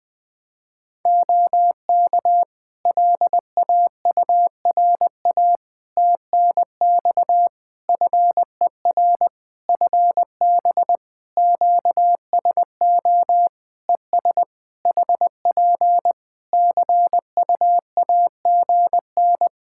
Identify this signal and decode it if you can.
Morse code, heard as a keyed tone, about 21 wpm, 700 Hz. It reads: OK LAURA TNX FER FB QSO ES HP CUAGN
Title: K1USN SST QSO Practice 20wpm